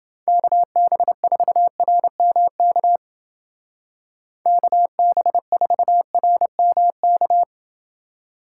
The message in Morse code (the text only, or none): KB4RMK KB4RMK